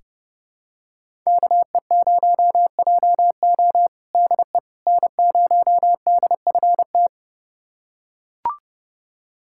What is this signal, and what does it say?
Morse code, heard as a keyed tone, about 30 wpm, 700 Hz. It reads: KE0JO DE N0DFT
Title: USA Callsign DE Callsign - 5 Characters - 30wpm